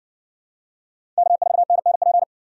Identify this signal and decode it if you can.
Morse code, heard as a keyed tone, about 53 wpm, 700 Hz. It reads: K4NNP